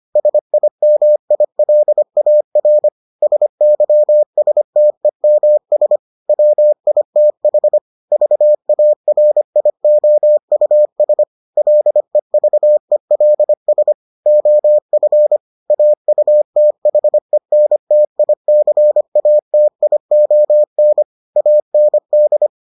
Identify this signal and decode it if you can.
Morse code, heard as a keyed tone, about 25 wpm, 600 Hz. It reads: SIMILAR SYSTEMS WITH VARIOUS LEVELS OF AUTHENTICATION AND